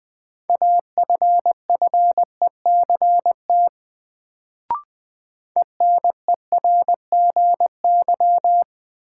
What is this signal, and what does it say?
Morse code, heard as a keyed tone, about 20 wpm, 700 Hz. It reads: AFFECT ENERGY
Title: Sentences from Top 900 Words - New Word Drill - 20wpm